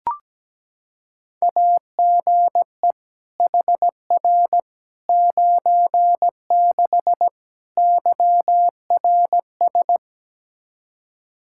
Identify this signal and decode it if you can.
Morse code, heard as a keyed tone, about 17 wpm, 700 Hz. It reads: AGE HR 96 YRS
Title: K1USN SST QSO Practice 17wpm